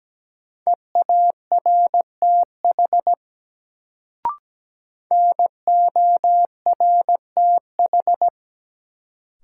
Morse code, heard as a keyed tone, about 17 wpm, 700 Hz. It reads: EARTH NORTH
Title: Sets of 5 Words Encore - New Word Drill - 17wpm